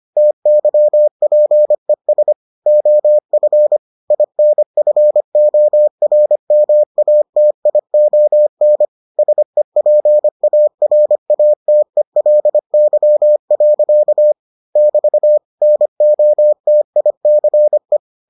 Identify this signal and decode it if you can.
Morse code, heard as a keyed tone, about 25 wpm, 600 Hz. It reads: TYPES OF INFORMATION SEPARATELY. <BT> NOTICE